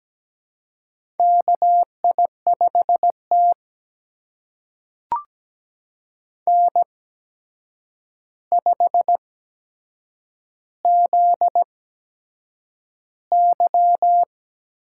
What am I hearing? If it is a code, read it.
Morse code, heard as a keyed tone, about 17 wpm, spaced as for 7 wpm, 700 Hz. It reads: KI5T N5ZY
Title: USA Call Signs - 4 Characters - 8x Character Spacing - 17wpm